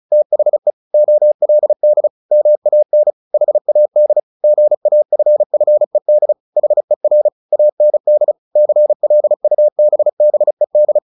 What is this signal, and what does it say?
Morse code, heard as a keyed tone, about 34 wpm, 600 Hz. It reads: THE OLD MAN HAD GAFFED HER AND CLUBBED
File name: the old man and the sea / the old man and the sea 1 35wpm